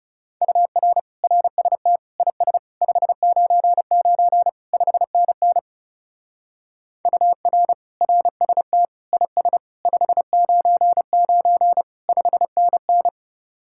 Morse code, heard as a keyed tone, about 39 wpm, 700 Hz. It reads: UR RST IS 599 5NN UR RST IS 599 5NN